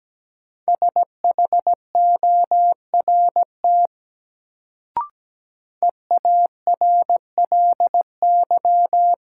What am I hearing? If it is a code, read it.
Morse code, heard as a keyed tone, about 17 wpm, 700 Hz. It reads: SHORT EARLY
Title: Sentences from Top 400 Words - New Word Drill - 17wpm